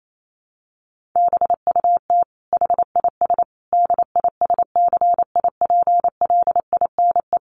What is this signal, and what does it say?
Morse code, heard as a keyed tone, about 28 wpm, 700 Hz. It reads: BUT HIS DISCIPLINE
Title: Sentences from Top 1400 Words 28wpm